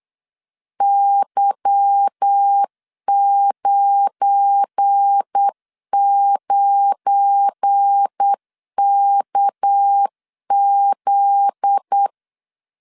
Morse code, heard as a keyed tone, about 8 wpm, 800 Hz. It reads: Y99KZ
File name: CW / DET / 1 9 1500C S B 8wpm 5624